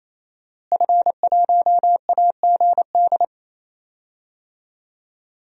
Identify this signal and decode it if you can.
Morse code, heard as a keyed tone, about 28 wpm, 700 Hz. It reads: F1AGD
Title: DXCC-100 Call Signs - 5 Characters - Repeated 3x - 28wpm